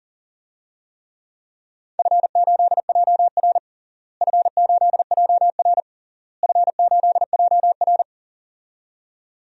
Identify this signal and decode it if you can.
Morse code, heard as a keyed tone, about 40 wpm, 700 Hz. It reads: F8JR F8JR F8JR
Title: DXCC-100 Call Signs - 4 Characters - Repeated 3x - 40wpm